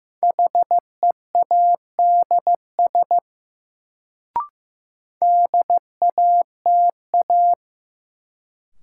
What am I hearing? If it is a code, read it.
Morse code, heard as a keyed tone, about 15 wpm, 700 Hz. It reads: HEADS DATA
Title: Sentences from Top 1000 Words - Review Part 3 - New Word Drill - 15wpm